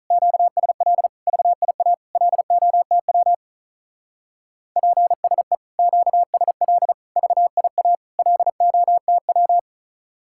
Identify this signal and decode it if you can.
Morse code, heard as a keyed tone, about 45 wpm, 700 Hz. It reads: QSL VIA LOTW PSE QSL VIA LOTW